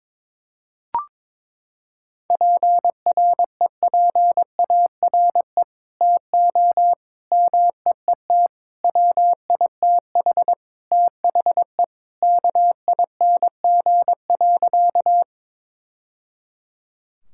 Morse code, heard as a keyed tone, about 22 wpm, 700 Hz. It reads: PREPARE TO MEET WITH THE KING.
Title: Sentences from Top 1000 Words - Review Part 6 - 22wpm